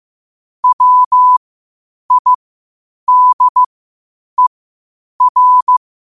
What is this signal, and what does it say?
Morse code, heard as a keyed tone, about 15 wpm, spaced as for 8 wpm, 1000 Hz. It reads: WIDER